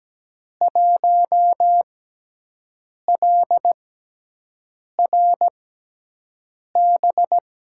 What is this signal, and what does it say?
Morse code, heard as a keyed tone, about 17 wpm, spaced as for 6 wpm, 700 Hz. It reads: 1LRB
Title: USA Call Signs - 5 Characters - 6x Character Spacing - 17wpm